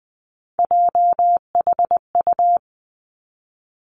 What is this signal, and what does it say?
Morse code, heard as a keyed tone, about 20 wpm, 700 Hz. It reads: JHU
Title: Three Challenging Letter-Number Combinations 20wpm